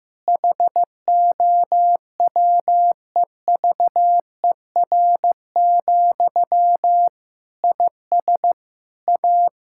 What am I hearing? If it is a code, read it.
Morse code, heard as a keyed tone, about 15 wpm, 700 Hz. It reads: HOWEVER, IS A
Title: Sentences from Top 1000 Words - Review Part 6 - 15wpm